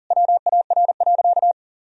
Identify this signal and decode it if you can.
Morse code, heard as a keyed tone, about 40 wpm, 700 Hz. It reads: WAR.
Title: Sentences from Top 1000 Words - Review Part 4 - 40wpm